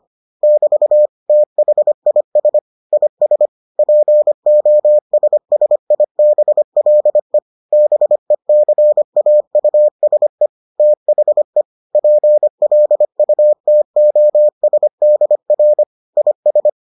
Morse code, heard as a keyed tone, about 25 wpm, 600 Hz. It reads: <BT> THIS IS POSSIBLE BECAUSE THE PLUTOSDR IS